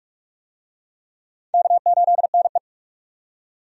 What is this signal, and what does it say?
Morse code, heard as a keyed tone, about 46 wpm, 700 Hz. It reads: K8NE
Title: USA Call Signs - 4 Characters - 45wpm